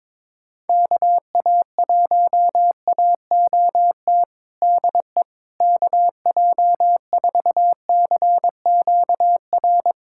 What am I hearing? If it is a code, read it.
Morse code, heard as a keyed tone, about 22 wpm, 700 Hz. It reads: KA1AOT DE KJ4CQR
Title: USA Callsign DE Callsign 22wpm